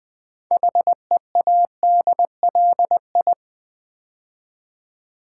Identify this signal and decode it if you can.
Morse code, heard as a keyed tone, about 20 wpm, 700 Hz. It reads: HEADLI
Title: Build A Word - 3 Letters At A Time - 8-Letter Words - 20wpm